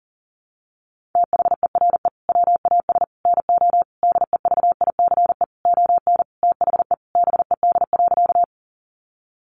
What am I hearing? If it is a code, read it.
Morse code, heard as a keyed tone, about 40 wpm, 700 Hz. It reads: THERE WAS NO DEVICE ON THE BED.
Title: Sentences from Top 1500 Words 40wpm